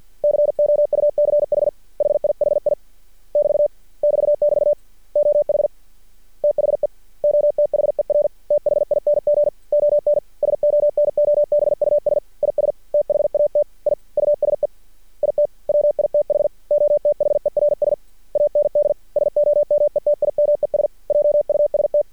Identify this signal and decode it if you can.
Morse code, heard as a keyed tone, about 50 wpm, 590 Hz. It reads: COUCH HIHI <BT> <BT><BT> OH THE OTHER THING ON SONOBUS IS THAT I USE IT WITH OTHERS AND SOMETIMES JUST
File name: archived / CRSnet